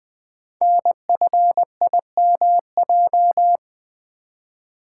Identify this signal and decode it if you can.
Morse code, heard as a keyed tone, about 20 wpm, 700 Hz. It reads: NFIMJ
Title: Code Groups - 5 Letters - 20wpm